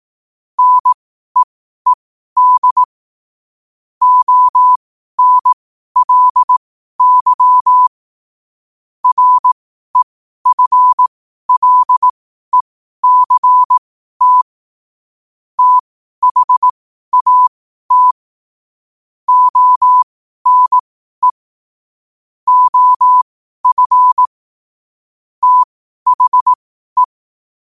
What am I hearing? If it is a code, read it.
Morse code, heard as a keyed tone, about 18 wpm, spaced as for 12 wpm, 1000 Hz. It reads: NEED ONLY REFLECT THAT ONE OF THE
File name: qotdfiles / 12wpm